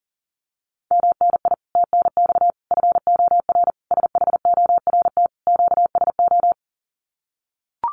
Morse code, heard as a keyed tone, about 38 wpm, 700 Hz. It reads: MNI TNX FOR SHORT QSO
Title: Common QSO Phrases 40wpm